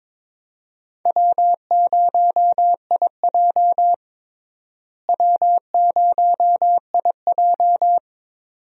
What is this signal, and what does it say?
Morse code, heard as a keyed tone, about 22 wpm, 700 Hz. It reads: W0IJ W0IJ
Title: USA Call Signs - 4 Characters - Repeated 2x - 22wpm